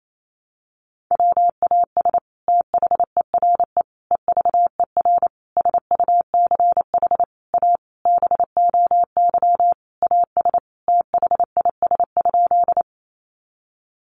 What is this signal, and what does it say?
Morse code, heard as a keyed tone, about 28 wpm, 700 Hz. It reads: WAS THERE EVER SUCH A BOY AS THIS?
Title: Sentences from Top 400 Words 28wpm